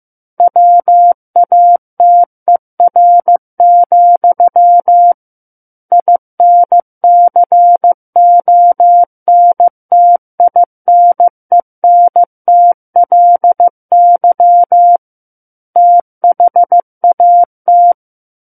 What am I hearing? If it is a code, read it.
Morse code, heard as a keyed tone, about 15 wpm, 700 Hz. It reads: WATER, INCONTINENTLY THAT
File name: Chapter 6 - 15 WPM